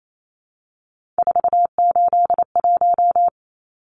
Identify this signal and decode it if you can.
Morse code, heard as a keyed tone, about 28 wpm, 700 Hz. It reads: 481
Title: Numbers - 3-Digit 28wpm